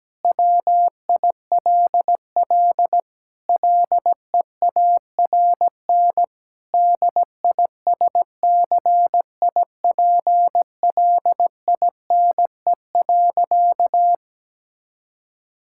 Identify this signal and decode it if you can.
Morse code, heard as a keyed tone, about 17 wpm, 700 Hz. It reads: WILL LEARN DISCIPLINE.
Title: Sentences from Top 1000 Words - Review Part 6 - 17wpm